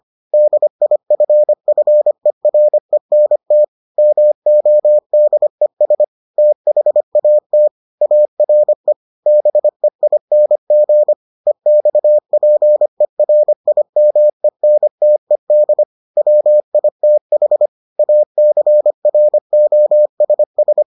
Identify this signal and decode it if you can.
Morse code, heard as a keyed tone, about 25 wpm, 600 Hz. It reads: DIFFERENT MODES THAT ARE BEING EXPERIMENTED WITH ACROSS